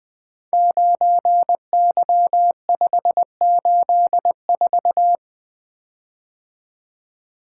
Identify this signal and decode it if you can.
Morse code, heard as a keyed tone, about 20 wpm, 700 Hz. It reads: 9Y584
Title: Code Groups - 5 Letter:Numbers - 20wpm